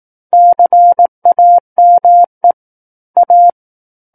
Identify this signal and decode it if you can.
Morse code, heard as a keyed tone, about 18 wpm, 700 Hz. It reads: CAME A
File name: 18 WPM